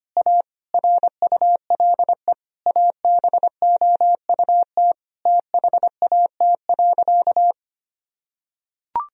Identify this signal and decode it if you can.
Morse code, heard as a keyed tone, about 25 wpm, 700 Hz. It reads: A RULE ABOUT THAT.
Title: Sentences from Top 1000 Words - Review Part 4 - 25wpm